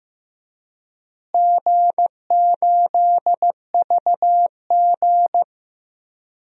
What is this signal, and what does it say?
Morse code, heard as a keyed tone, about 15 wpm, 700 Hz. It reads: G8VG
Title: DXCC-100 Call Signs - 4 Characters - Repeated 3x - 15wpm